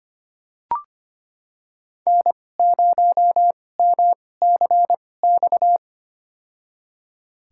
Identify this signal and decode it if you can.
Morse code, heard as a keyed tone, about 25 wpm, spaced as for 18 wpm, 700 Hz. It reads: N0MCX
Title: USA Call Signs - 5 Characters - 2x Character Spacing - 25wpm